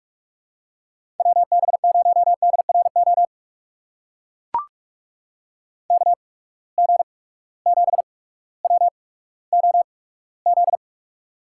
Morse code, heard as a keyed tone, about 45 wpm, spaced as for 19 wpm, 700 Hz. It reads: WB0DRO KG7WOZ
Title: USA Call Signs - 6 Characters - 8x Character Spacing - 45wpm